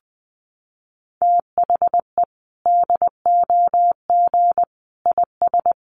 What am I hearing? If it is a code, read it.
Morse code, heard as a keyed tone, about 20 wpm, 700 Hz. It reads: THE DOG IS
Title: Sentences from Top 400 Words 20wpm